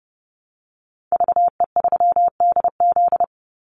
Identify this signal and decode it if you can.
Morse code, heard as a keyed tone, about 29 wpm, 700 Hz. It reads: VE3DZ
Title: Call Signs Repeated 3x - 30wpm